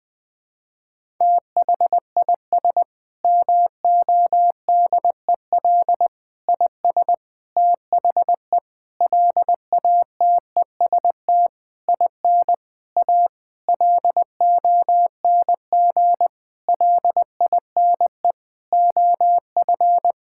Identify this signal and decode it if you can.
Morse code, heard as a keyed tone, about 20 wpm, 700 Hz. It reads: THIS MODEL IS THE LATEST IN A LONG LINE OF
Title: Sentences with 8-Word Phrases - 20wpm